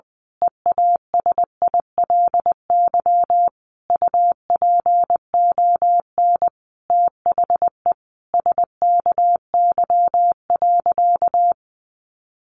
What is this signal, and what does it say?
Morse code, heard as a keyed tone, about 20 wpm, 700 Hz. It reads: EASILY UPON THE SKY.
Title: A Sound Of Thunder 20wpm